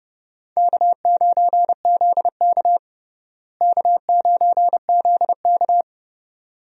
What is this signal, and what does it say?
Morse code, heard as a keyed tone, about 30 wpm, 700 Hz. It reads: K9ZK K9ZK
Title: USA Call Signs - 4 Characters - Repeated 2x - 30wpm